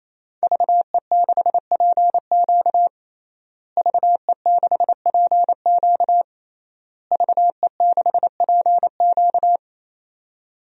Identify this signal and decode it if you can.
Morse code, heard as a keyed tone, about 27 wpm, 700 Hz. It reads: VE6PQ VE6PQ VE6PQ
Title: DXCC-100 Call Signs - 5 Characters - Repeated 3x - 28wpm